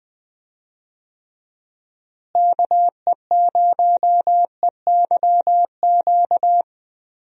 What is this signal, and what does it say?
Morse code, heard as a keyed tone, about 21 wpm, 700 Hz. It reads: KE0EYQ
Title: USA Call Signs - 6 Characters - Repeated 2x - 20wpm